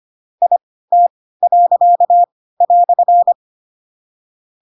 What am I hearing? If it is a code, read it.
Morse code, heard as a keyed tone, about 25 wpm, spaced as for 16 wpm, 700 Hz. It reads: IT."
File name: Pinocchio / Pinocchio ch4 10 25 ik1hge